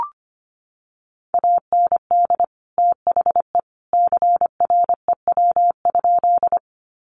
Morse code, heard as a keyed tone, about 25 wpm, 700 Hz. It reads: AND THE CREW?
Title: Sentences from Top 1500 Words 25wpm